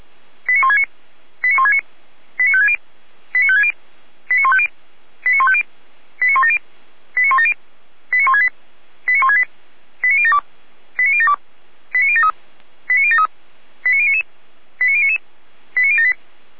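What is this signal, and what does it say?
Five-tone selective call (ZVEI), heard as quick runs of five tones, tones of about 70 ms. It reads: 89169 89169 89560 89560 89150 89150 89170 89170 89168 89168 89072 89072 89073 89073 89099 89099 89078
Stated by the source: ZVEI five-tone selective call received on an SDR by Feuerwehr Orlamünde.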